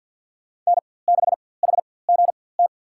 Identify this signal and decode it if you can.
Morse code, heard as a keyed tone, about 51 wpm, spaced as for 23 wpm, 700 Hz. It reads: N6HGT
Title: USA Call Signs - 4x Character Spacing - 50wpm